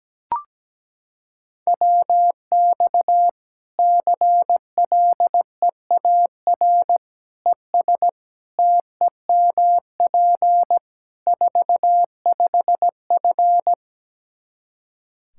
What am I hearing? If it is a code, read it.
Morse code, heard as a keyed tone, about 17 wpm, 700 Hz. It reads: WX CLEAR ES TEMP 45F
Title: Practice QSO - Exchange 2 - Info - 17wpm